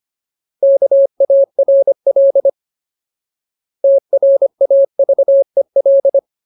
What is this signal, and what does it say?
Morse code, heard as a keyed tone, about 25 wpm, 550 Hz. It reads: KARL TRAVEL
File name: Ultimate Word List / 25WPM Ultimate